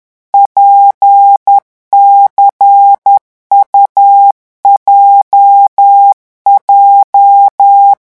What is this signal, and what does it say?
Morse code, heard as a keyed tone, about 10 wpm, 800 Hz. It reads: PCUJJ
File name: morse mp3 / revision8-vitesse 10 mots